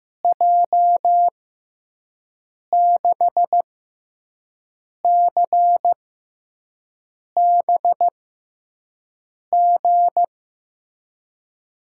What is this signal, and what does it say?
Morse code, heard as a keyed tone, about 15 wpm, spaced as for 5 wpm, 700 Hz. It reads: J6CBG
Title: USA Call Signs - 6x Character Spacing - 15wpm